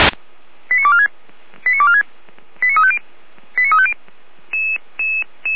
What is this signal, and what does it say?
Five-tone selective call (ZVEI), heard as quick runs of five tones, tones of about 70 ms. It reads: 89246 89246 89350 89350